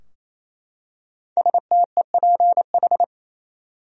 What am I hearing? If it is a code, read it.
Morse code, heard as a keyed tone, about 27 wpm, 700 Hz. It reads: STEPH
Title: USA Common Names - 28wpm